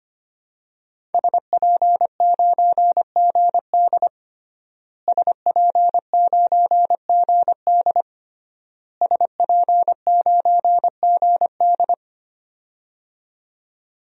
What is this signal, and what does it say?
Morse code, heard as a keyed tone, about 25 wpm, 700 Hz. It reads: SP9GD SP9GD SP9GD
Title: DXCC-100 Call Signs - 5 Characters - Repeated 3x - 25wpm